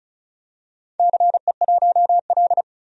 Morse code, heard as a keyed tone, about 34 wpm, 700 Hz. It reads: CE1L
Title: DXCC-100 Call Signs - 4 Characters - 35wpm